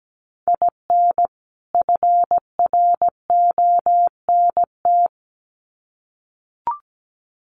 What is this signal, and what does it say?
Morse code, heard as a keyed tone, about 17 wpm, 700 Hz. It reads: IN FRONT
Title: Sets of 2 Words 17wpm